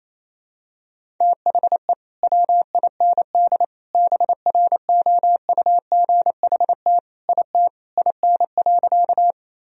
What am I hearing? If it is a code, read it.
Morse code, heard as a keyed tone, about 28 wpm, 700 Hz. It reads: THE WIND BROUGHT IT IN.
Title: Sentences from Top 1000 Words - Review Part 2 - 28wpm